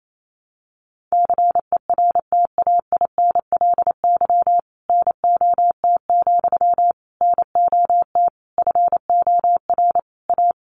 Morse code, heard as a keyed tone, about 28 wpm, 700 Hz. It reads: CERTAINLY NOT, NOT FOR A
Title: Sentences from Top 900 Words 28wpm